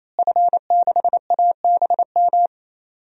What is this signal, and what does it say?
Morse code, heard as a keyed tone, about 27 wpm, 700 Hz. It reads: F6ABM
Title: DXCC-100 Call Signs - 5 Characters - Repeated 3x - 28wpm